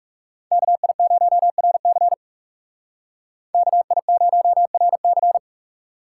Morse code, heard as a keyed tone, about 39 wpm, 700 Hz. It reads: KI0RC KI0RC